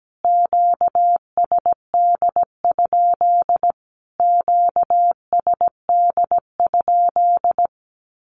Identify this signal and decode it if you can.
Morse code, heard as a keyed tone, about 17 wpm, 700 Hz. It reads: QSD? QSD?
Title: Q-Codes Repeated 3x 17wpm